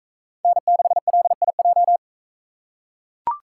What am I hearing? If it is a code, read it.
Morse code, heard as a keyed tone, about 42 wpm, 700 Hz. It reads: N6LIJ